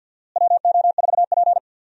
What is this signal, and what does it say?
Morse code, heard as a keyed tone, about 53 wpm, 700 Hz. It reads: WO4P
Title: USA Call Signs - 4 Characters - Repeated 3x - 50wpm